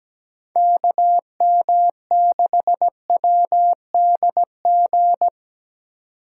Morse code, heard as a keyed tone, about 17 wpm, 700 Hz. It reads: KM6WDG
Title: USA Call Signs - 6 Characters - 4x Character Spacing - 17wpm